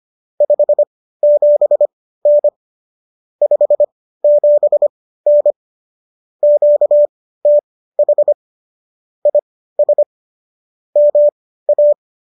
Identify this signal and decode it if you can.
Morse code, heard as a keyed tone, about 25 wpm, spaced as for 15 wpm, 590 Hz. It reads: 57N 57N QTH IS MA